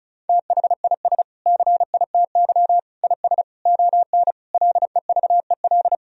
Morse code, heard as a keyed tone, about 34 wpm, 700 Hz. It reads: THIS CITY IS ON LEVEL
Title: Sentences from Top 1000 Words - Review Part 4 - 35wpm